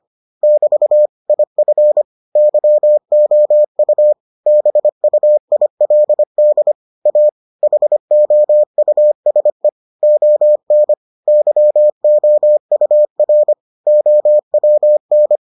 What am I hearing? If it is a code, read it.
Morse code, heard as a keyed tone, about 25 wpm, 600 Hz. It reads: <BT> IF YOU BUILD A HOUSE ON YOUR OWN